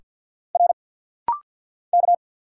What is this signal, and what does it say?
Morse code, heard as a keyed tone, about 47 wpm, 690 Hz. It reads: R K
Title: Single Letter-Number Alt - Rapid-Fire 50wpm